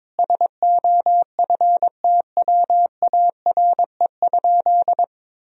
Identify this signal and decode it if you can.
Morse code, heard as a keyed tone, about 22 wpm, 700 Hz. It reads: SOFTWARE?
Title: Sentences from Top 1000 Words - Review Part 3 - 22wpm